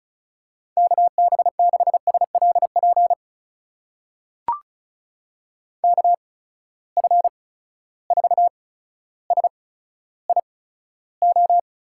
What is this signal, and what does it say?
Morse code, heard as a keyed tone, about 36 wpm, spaced as for 15 wpm, 700 Hz. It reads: KB6SLP KF4SIO
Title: USA Call Signs - 6 Characters - 8x Character Spacing - 35wpm